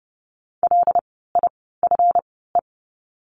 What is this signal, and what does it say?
Morse code, heard as a keyed tone, about 30 wpm, spaced as for 17 wpm, 700 Hz. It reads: LIFE
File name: course / 30:10wpm / 230 Lesson 37f - Letter x - Words 30wpm